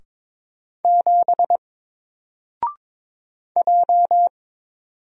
Morse code, heard as a keyed tone, about 22 wpm, 700 Hz. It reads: 7 J
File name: Single-Letter-Number-22wpm